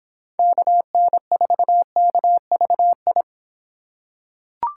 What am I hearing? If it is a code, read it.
Morse code, heard as a keyed tone, about 26 wpm, 700 Hz. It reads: KN4KVI